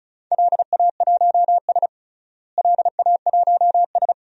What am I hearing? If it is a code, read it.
Morse code, heard as a keyed tone, about 34 wpm, 700 Hz. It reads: LA1S LA1S
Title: DXCC-100 Call Signs - 4 Characters - Repeated 2x - 35wpm